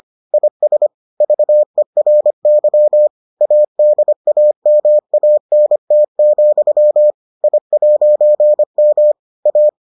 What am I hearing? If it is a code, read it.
Morse code, heard as a keyed tone, about 25 wpm, 600 Hz. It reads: IS VERY ADAMANT, I'M A